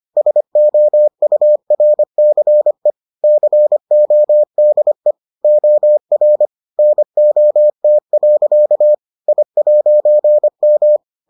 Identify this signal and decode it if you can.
Morse code, heard as a keyed tone, about 25 wpm, 600 Hz. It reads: SOURCE CODE OR NOT. I'M